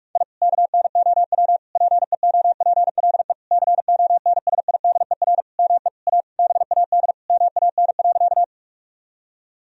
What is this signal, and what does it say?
Morse code, heard as a keyed tone, about 41 wpm, 700 Hz. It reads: I KNOW PEOPLE CONSIDER ME A BAD MAN.
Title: Sentences from Top 1000 Words - Review Part 4 - 45wpm